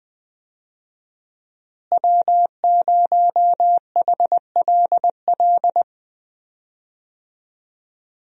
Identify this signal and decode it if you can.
Morse code, heard as a keyed tone, about 20 wpm, 700 Hz. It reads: W0HLL
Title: USA Call Signs - 5 Characters - Repeated 2x - 20wpm